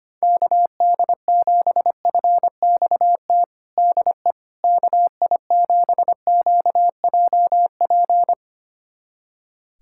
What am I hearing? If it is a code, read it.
Morse code, heard as a keyed tone, about 25 wpm, 700 Hz. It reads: KD7FXT DE KI7QJP
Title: USA Callsign DE Callsign - 6 Characters - 25wpm